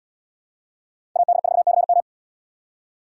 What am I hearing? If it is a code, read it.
Morse code, heard as a keyed tone, about 70 wpm, 700 Hz. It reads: AD4CM